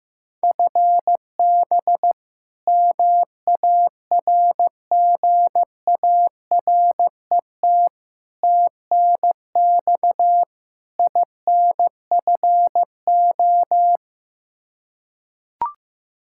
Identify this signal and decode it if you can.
Morse code, heard as a keyed tone, about 15 wpm, 700 Hz. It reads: FB MARGARET TNX INFO
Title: Practice QSO - Exchange 3 - More Info - 15wpm